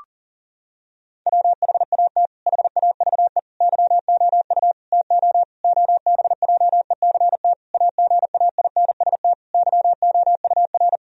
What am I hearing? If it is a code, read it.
Morse code, heard as a keyed tone, about 37 wpm, 700 Hz. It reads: WHAT HAVE YOU TO OBJECT AGAINST YOUR
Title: Sentences from Top 1000 Words - Review Part 3 - 40wpm